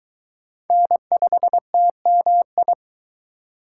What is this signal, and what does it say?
Morse code, heard as a keyed tone, about 23 wpm, 700 Hz. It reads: N5TMI